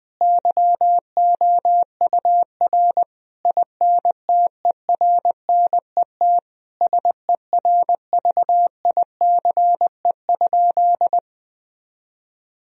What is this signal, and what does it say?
Morse code, heard as a keyed tone, about 21 wpm, 700 Hz. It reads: YOUR INTERNET SERVICE?
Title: Sentences from Top 1000 Words - Review Part 4 - 20wpm